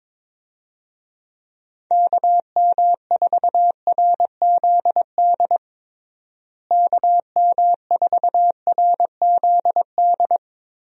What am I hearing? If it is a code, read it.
Morse code, heard as a keyed tone, about 22 wpm, 700 Hz. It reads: KM4RZD KM4RZD
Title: USA Call Signs - 6 Characters - Repeated 2x - 22wpm